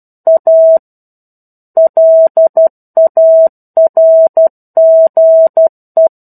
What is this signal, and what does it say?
Morse code, heard as a keyed tone, about 12 wpm, 650 Hz. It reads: A LARGE
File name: 01-Chapter-01 12 WPM